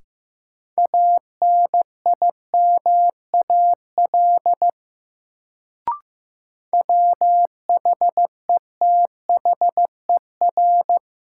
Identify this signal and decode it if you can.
Morse code, heard as a keyed tone, about 15 wpm, 700 Hz. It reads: ANIMAL WHETHER
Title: Sentences from Top 1300 Words - New Word Drill Alt - 15wpm